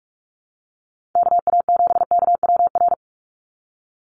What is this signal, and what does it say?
Morse code, heard as a keyed tone, about 46 wpm, 690 Hz. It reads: KA7KWR
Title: USA Call Signs 45wpm